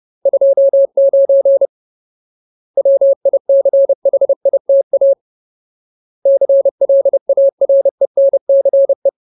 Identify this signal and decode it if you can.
Morse code, heard as a keyed tone, about 30 wpm, 550 Hz. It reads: 29 WICHITA CLARENCE
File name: Ultimate Word List / 30WPM Ultimate